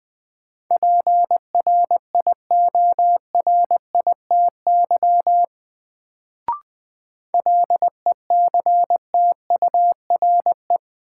Morse code, heard as a keyed tone, about 20 wpm, 700 Hz. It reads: PRIORITY LECTURE
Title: Sentences from Top 1600 Words - New Word Drill Alt - 20wpm